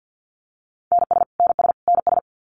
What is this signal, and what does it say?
Morse code, heard as a keyed tone, about 50 wpm, 700 Hz. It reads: NS NS NS
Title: Canadian Provinces and Territories - Repeated 3x - 50wpm